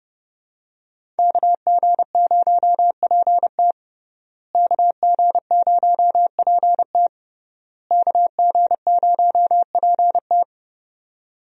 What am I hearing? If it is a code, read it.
Morse code, heard as a keyed tone, about 30 wpm, 700 Hz. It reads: KG0PT KG0PT KG0PT
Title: USA Call Signs - 5 Characters - Repeated 3x - 30wpm